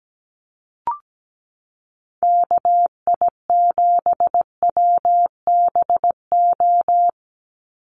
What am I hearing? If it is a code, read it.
Morse code, heard as a keyed tone, about 17 wpm, 700 Hz. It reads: KI7WBO
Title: USA Call Signs 17wpm